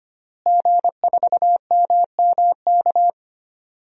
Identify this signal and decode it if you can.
Morse code, heard as a keyed tone, about 25 wpm, 700 Hz. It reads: G4MMK
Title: DXCC-100 Call Signs - 5 Characters - Repeated 3x - 25wpm